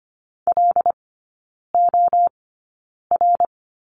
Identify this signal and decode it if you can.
Morse code, heard as a keyed tone, about 25 wpm, spaced as for 9 wpm, 690 Hz. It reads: LOR
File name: Lesson 15e - Number 5 - Three Characters 25wpm